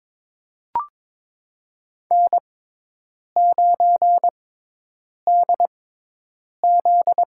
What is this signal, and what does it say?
Morse code, heard as a keyed tone, about 22 wpm, spaced as for 8 wpm, 700 Hz. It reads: N9DZ
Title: USA Call Signs - 4 Characters - 6x Character Spacing - 22wpm